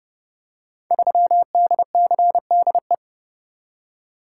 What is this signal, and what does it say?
Morse code, heard as a keyed tone, about 30 wpm, 700 Hz. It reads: 3DCDE
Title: Code Groups - 5 Letter:Numbers - 30wpm